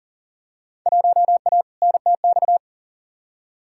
Morse code, heard as a keyed tone, about 40 wpm, 700 Hz. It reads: 1A NTX
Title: Field Day Exchanges - 40wpm